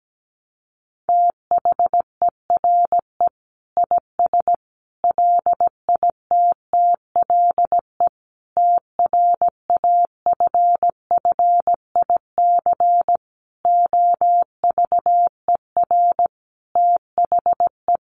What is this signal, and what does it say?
Morse code, heard as a keyed tone, about 17 wpm, 700 Hz. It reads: THERE IS LITTLE TRAFFIC OVER THE
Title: Sentences from Top 1300 Words 17wpm